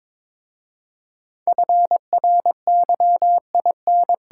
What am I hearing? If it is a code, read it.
Morse code, heard as a keyed tone, about 22 wpm, 700 Hz. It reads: FRYIN
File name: practice / 22wpm / Difficult Words with Hints - 22wpm